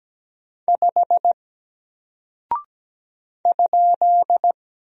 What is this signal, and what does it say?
Morse code, heard as a keyed tone, about 17 wpm, 700 Hz. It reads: 5 ?
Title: Single Character (Letter:Number:Punctuation:Prosign) Alt 17wpm